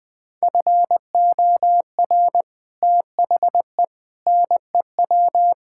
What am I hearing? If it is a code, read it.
Morse code, heard as a keyed tone, about 20 wpm, 700 Hz. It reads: FOR THE NEW
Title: Build A Phrase - 5-Word Phrases - 20wpm